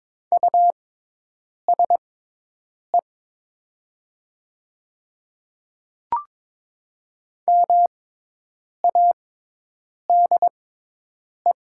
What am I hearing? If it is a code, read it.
Morse code, heard as a keyed tone, about 22 wpm, spaced as for 8 wpm, 700 Hz. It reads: USE MADE
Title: 6x Character Spacing - Top 200 Words - 22wpm